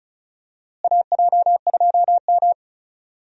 Morse code, heard as a keyed tone, about 34 wpm, 700 Hz. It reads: AJ2M
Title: USA Call Signs - 4 Characters - Repeated 3x - 35wpm